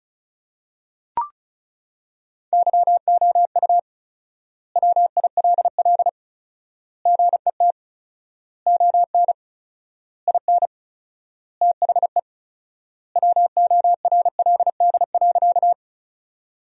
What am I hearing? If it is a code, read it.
Morse code, heard as a keyed tone, about 36 wpm, 700 Hz. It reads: YOU WILL GET ON IN THE WORLD.
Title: 4x Word-Spacing - Sentences from Top 200 Words - 35wpm